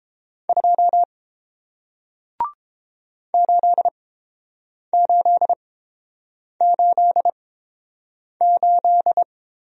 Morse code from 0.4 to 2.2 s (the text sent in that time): 2